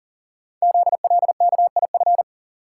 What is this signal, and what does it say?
Morse code, heard as a keyed tone, about 40 wpm, 700 Hz. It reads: ZLKIF
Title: Code Groups - 5 Letters - 40wpm